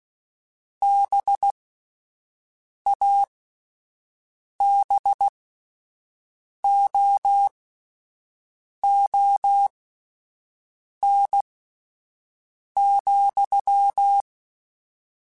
Morse code, heard as a keyed tone, about 16 wpm, spaced as for 5 wpm, 800 Hz. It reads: BABOON,